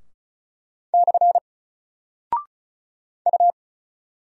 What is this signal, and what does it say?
Morse code, heard as a keyed tone, about 36 wpm, 700 Hz. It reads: / U
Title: Single Character (Letter:Number:Punctuation:Prosign) 35wpm